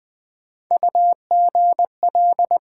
Morse code, heard as a keyed tone, about 20 wpm, 700 Hz. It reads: UGL
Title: Build A Word - 1 Letter At A Time - 3 to 5-Letter Words - 20wpm